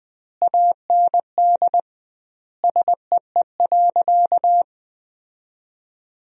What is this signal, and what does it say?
Morse code, heard as a keyed tone, about 20 wpm, 700 Hz. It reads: AND SEE.
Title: 2x Word-Spacing - Sentences from Top 100 Words - 20wpm